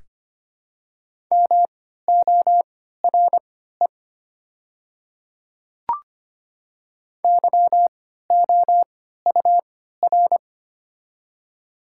Morse code, heard as a keyed tone, about 25 wpm, spaced as for 14 wpm, 700 Hz. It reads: MORE YOUR
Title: 3x Character Spacing - Top 100 Words - 25wpm